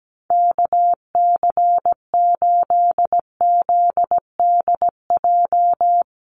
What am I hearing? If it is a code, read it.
Morse code, heard as a keyed tone, about 17 wpm, 700 Hz. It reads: KC8ZDJ
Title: USA Call Signs 17wpm